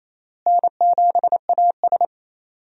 Morse code, heard as a keyed tone, about 27 wpm, 700 Hz. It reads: N7AS
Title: USA Call Signs - 4 Characters - Repeated 3x - 28wpm